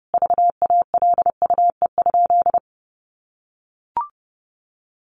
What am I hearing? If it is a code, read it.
Morse code, heard as a keyed tone, about 30 wpm, 700 Hz. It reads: VALUE?
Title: Sentences from Top 1400 Words 30wpm